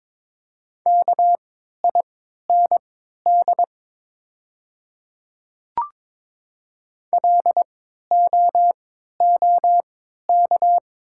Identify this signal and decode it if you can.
Morse code, heard as a keyed tone, about 22 wpm, spaced as for 12 wpm, 700 Hz. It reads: KIND LOOK
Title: 3x Character Spacing - Top 200 Words - 22wpm